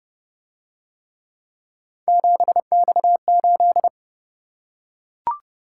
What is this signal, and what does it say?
Morse code, heard as a keyed tone, about 27 wpm, 700 Hz. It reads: 7X8